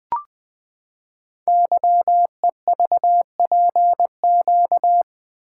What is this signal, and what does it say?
Morse code, heard as a keyed tone, about 20 wpm, 700 Hz. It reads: YEVPQ
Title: Code Groups - 5 Letters - 20wpm